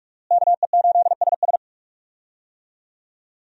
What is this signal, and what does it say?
Morse code, heard as a keyed tone, about 46 wpm, 700 Hz. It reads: KE8SS
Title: USA Call Signs - 5 Characters - Repeated 2x - 45wpm